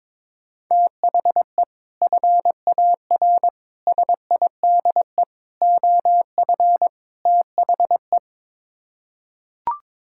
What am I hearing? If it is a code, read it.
Morse code, heard as a keyed tone, about 22 wpm, 700 Hz. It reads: THE FAR SIDE OF THE
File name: practice / 22wpm / Build A Phrase - 5-Word Phrases - 22wpm